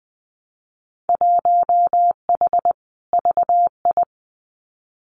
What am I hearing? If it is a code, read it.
Morse code, heard as a keyed tone, about 20 wpm, 700 Hz. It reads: 1H VI
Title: Winter Field Day Exchanges Encore 20wpm